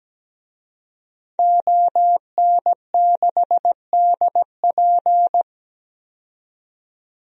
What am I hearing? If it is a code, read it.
Morse code, heard as a keyed tone, about 17 wpm, 700 Hz. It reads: ON6DP
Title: DXCC-100 Call Signs - 5 Characters - 17wpm